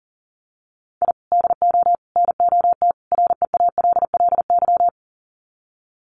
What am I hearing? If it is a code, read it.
Morse code, heard as a keyed tone, about 40 wpm, 700 Hz. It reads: I DO NOT REALLY
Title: Sets of 4 Words 40wpm